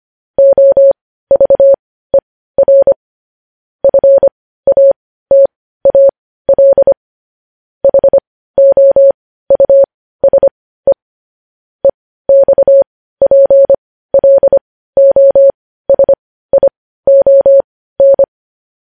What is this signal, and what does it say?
Morse code, heard as a keyed tone, about 25 wpm, spaced as for 15 wpm, 550 Hz. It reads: OVER FATAL HOUSE EXPLOSION
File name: podcast / News headlines-1x-25x15